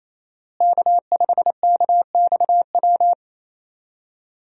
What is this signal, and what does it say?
Morse code, heard as a keyed tone, about 28 wpm, 700 Hz. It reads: K5KXW
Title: USA Call Signs - 5 Characters - Repeated 3x - 28wpm